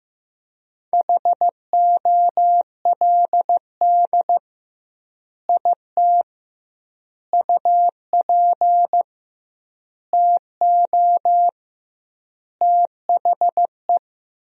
Morse code, heard as a keyed tone, about 15 wpm, 700 Hz. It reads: HOLD IT UP TO THE
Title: 2x Word-Spacing - Sentences from Top 400 Words - 15wpm